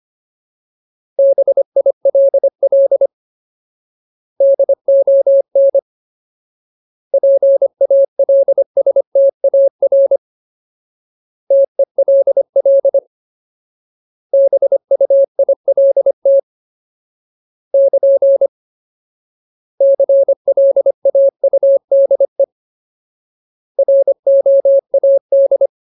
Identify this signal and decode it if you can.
Morse code, heard as a keyed tone, about 25 wpm, 550 Hz. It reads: BILL DON PALSTAR TELL BUILT <KN> CLAUDE ROAD
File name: Ultimate Word List / 25WPM Ultimate